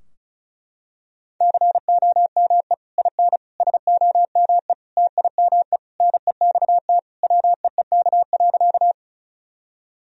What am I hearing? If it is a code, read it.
Morse code, heard as a keyed tone, about 34 wpm, 700 Hz. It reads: COME IN SOME TIME NEXT WEEK.
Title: Sentences from Top 1000 Words - Review Part 2 - 35wpm